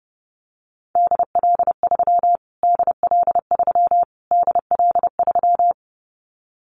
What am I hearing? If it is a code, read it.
Morse code, heard as a keyed tone, about 30 wpm, 700 Hz. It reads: DL3 DL3 DL3
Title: Callsign Prefixes for Top 25 DXCC - Repeated 3x - 30wpm